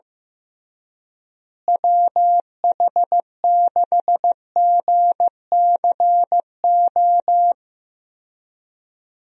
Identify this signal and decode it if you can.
Morse code, heard as a keyed tone, about 15 wpm, 700 Hz. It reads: WH6GCO
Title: USA Call Signs - 6 Characters - Repeated 3x - 15wpm